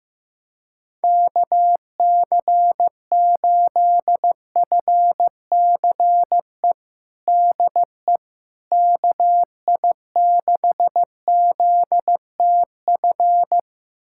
Morse code, heard as a keyed tone, about 15 wpm, 700 Hz. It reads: KC8FCE DE KI6ZTF
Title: USA Callsign DE Callsign - 6 Characters - 15wpm